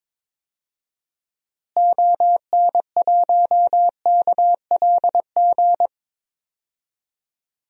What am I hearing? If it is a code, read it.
Morse code, heard as a keyed tone, about 22 wpm, 700 Hz. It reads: ON1KLG
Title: DXCC-100 Call Signs - 6 Characters - Repeated 3x - 22wpm